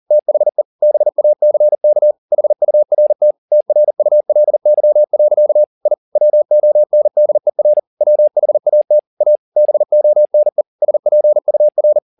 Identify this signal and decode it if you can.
Morse code, heard as a keyed tone, about 40 wpm, 600 Hz. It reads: THE BACK HURT TRULY. I WONDER WHAT A BONE SPUR
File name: the old man and the sea 2 40wpm